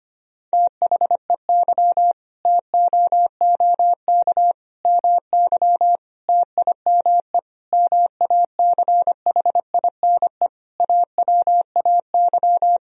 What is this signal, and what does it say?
Morse code, heard as a keyed tone, about 25 wpm, 700 Hz. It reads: THEY TOOK MY TIME MACHINE AWAY
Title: Sentences from Top 1000 Words - Review Part 2 - 25wpm